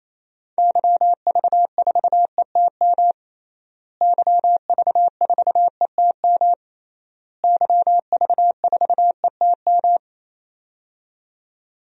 Morse code, heard as a keyed tone, about 28 wpm, 700 Hz. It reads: YV4ETM YV4ETM YV4ETM
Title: DXCC-100 Call Signs - 6 Characters - Repeated 3x - 28wpm